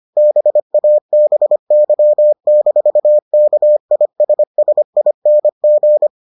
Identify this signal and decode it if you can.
Morse code, heard as a keyed tone, about 25 wpm, 600 Hz. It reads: BABY-KISSING